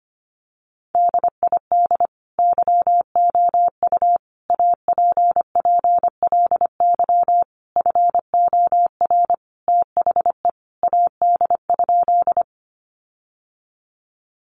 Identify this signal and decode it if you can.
Morse code, heard as a keyed tone, about 25 wpm, 700 Hz. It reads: DID YOU APPLY FOR THE AD?
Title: Sentences from Top 1300 Words 25wpm